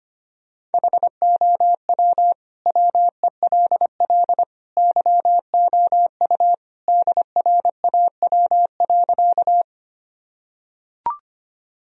Morse code, heard as a keyed tone, about 25 wpm, 700 Hz. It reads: HOW WELL YOU DRAW.
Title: Sentences from Top 1000 Words - Review Part 1 - 25wpm